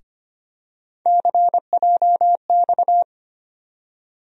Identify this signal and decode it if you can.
Morse code, heard as a keyed tone, about 25 wpm, 700 Hz. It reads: CJX
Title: USA Call Sign Postfixes - 3 Letters 25wpm